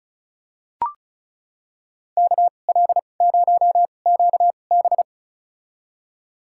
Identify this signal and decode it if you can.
Morse code, heard as a keyed tone, about 36 wpm, spaced as for 25 wpm, 700 Hz. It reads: KL0QB
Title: USA Call Signs - 2x Character Spacing - 35wpm